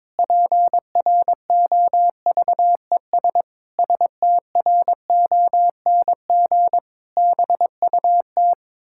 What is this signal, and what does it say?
Morse code, heard as a keyed tone, about 22 wpm, 700 Hz. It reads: PROVES STRONG BUT
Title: Pangram Challenge v2 - 22wpm